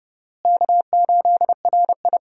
Morse code, heard as a keyed tone, about 30 wpm, 700 Hz. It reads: K8RI
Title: USA Call Signs - 4 Characters - Repeated 3x - 30wpm